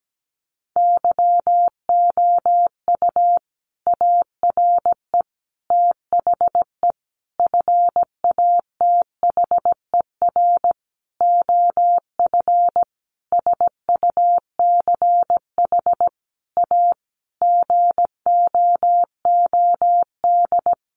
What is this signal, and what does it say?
Morse code, heard as a keyed tone, about 17 wpm, 700 Hz. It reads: YOU ARE THE FATHER OF SUCH A GOOD
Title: Sentences from Top 1500 Words 17wpm